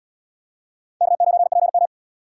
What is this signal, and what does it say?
Morse code, heard as a keyed tone, about 70 wpm, 700 Hz. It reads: K9PM